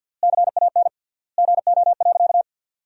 Morse code, heard as a keyed tone, about 50 wpm, 700 Hz. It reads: CAN GO.
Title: 3x Word-Spacing - Sentences from Top 100 Words - 50wpm